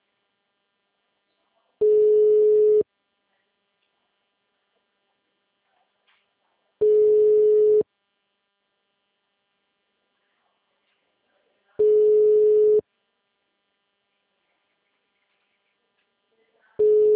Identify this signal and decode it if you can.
Ringback tone (Europe, 425 Hz).